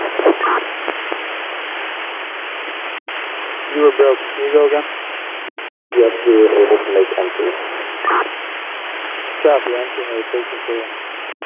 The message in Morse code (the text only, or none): none